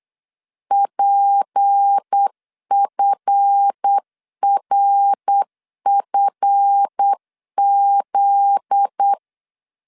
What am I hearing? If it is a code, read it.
Morse code, heard as a keyed tone, about 8 wpm, 800 Hz. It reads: PFRFZ
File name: CW / DET / 1 9 1500C S C 8wpm 5653